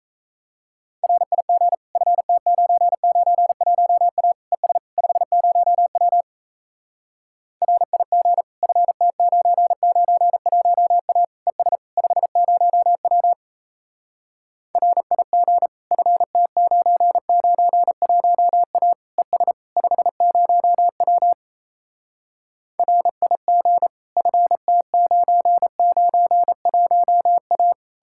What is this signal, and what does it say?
Morse code, heard as a keyed tone, about 37 wpm, 700 Hz. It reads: RIG FT991A ES 50W RIG FT991A ES 50W RIG FT991A ES 50W RIG FT991A